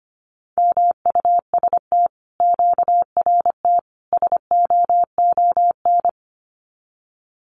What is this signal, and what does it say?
Morse code, heard as a keyed tone, about 25 wpm, 700 Hz. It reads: MUST QRT SOON
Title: Common QSO Phrases Encore 25wpm